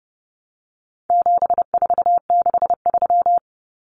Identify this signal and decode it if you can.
Morse code, heard as a keyed tone, about 29 wpm, 700 Hz. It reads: 7463
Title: Common Cut Numbers - 4-Digit 30wpm